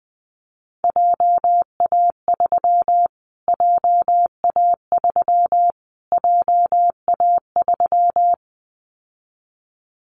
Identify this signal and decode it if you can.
Morse code, heard as a keyed tone, about 20 wpm, 700 Hz. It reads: JA3 JA3 JA3
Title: Callsign Prefixes for Top 25 DXCC - Repeated 3x - 20wpm